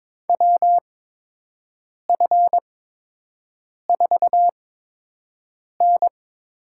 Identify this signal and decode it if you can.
Morse code, heard as a keyed tone, about 22 wpm, spaced as for 6 wpm, 700 Hz. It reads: WF4N
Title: USA Call Signs - 4 Characters - 8x Character Spacing - 22wpm